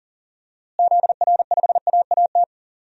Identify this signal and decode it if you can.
Morse code, heard as a keyed tone, about 40 wpm, 700 Hz. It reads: ZR5AAT
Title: DXCC-100 Call Signs - 6 Characters - Repeated 2x - 40wpm